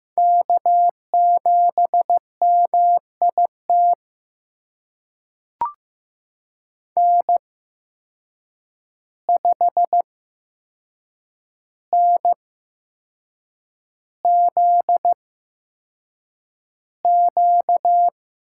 Morse code, heard as a keyed tone, about 15 wpm, spaced as for 6 wpm, 700 Hz. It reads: K7MIT N5NZQ
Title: USA Call Signs - 5 Characters - 8x Character Spacing - 15wpm